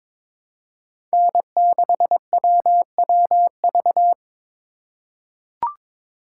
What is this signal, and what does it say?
Morse code, heard as a keyed tone, about 23 wpm, 700 Hz. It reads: N6WWV
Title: USA Call Signs - 5 Characters - 8x Character Spacing - 22wpm